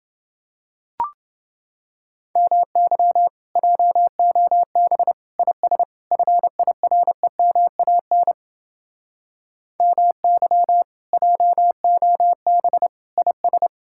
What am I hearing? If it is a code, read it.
Morse code, heard as a keyed tone, about 29 wpm, 700 Hz. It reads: MY JOB IS FIREMAN MY JOB IS